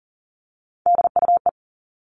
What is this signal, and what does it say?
Morse code, heard as a keyed tone, about 38 wpm, 700 Hz. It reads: DUE
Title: Three Challenging Letter-Number Combinations 40wpm